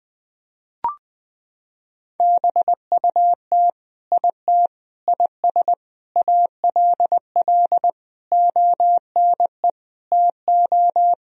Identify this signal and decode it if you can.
Morse code, heard as a keyed tone, about 19 wpm, 700 Hz. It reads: BUT IT IS ALL ONE TO
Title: Sentences from Top 1000 Words - Review Part 1 - 20wpm